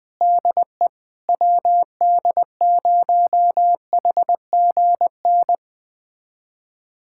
Morse code, heard as a keyed tone, about 20 wpm, 700 Hz. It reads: DE WD0HGN
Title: USA Callsign DE Callsign - 6 Characters - 20wpm